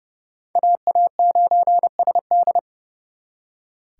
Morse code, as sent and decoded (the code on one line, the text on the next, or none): .- .- ----. ... -..
AA9SD